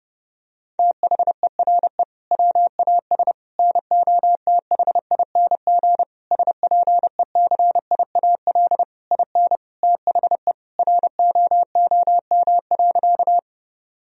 Morse code, heard as a keyed tone, about 29 wpm, 700 Hz. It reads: THERE WAS NOTHING SPECIAL IN THE ROOM.
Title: Sentences from Top 1000 Words - Review Part 3 - 30wpm